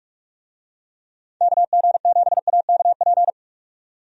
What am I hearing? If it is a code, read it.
Morse code, heard as a keyed tone, about 46 wpm, 700 Hz. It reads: KG7AKP
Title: USA Call Signs - 6 Characters - 45wpm